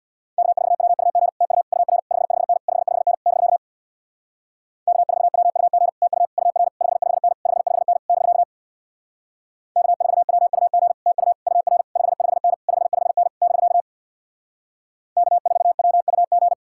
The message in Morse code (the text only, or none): K4WUG TU UR 55N 55N <BK> K4WUG TU UR 55N 55N <BK> K4WUG TU UR 55N 55N <BK> K4WUG